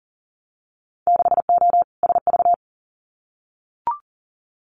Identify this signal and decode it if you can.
Morse code, heard as a keyed tone, about 40 wpm, 700 Hz. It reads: 6O SV
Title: Winter Field Day Exchanges Encore 40wpm